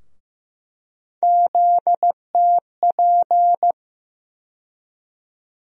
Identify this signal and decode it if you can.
Morse code, heard as a keyed tone, about 15 wpm, 700 Hz. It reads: ZTP
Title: USA Call Sign Postfixes - 3 Letters 15wpm